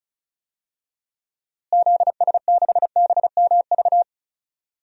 Morse code, heard as a keyed tone, about 34 wpm, 700 Hz. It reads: ZS6BMV
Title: DXCC-100 Call Signs - 6 Characters - Repeated 3x - 35wpm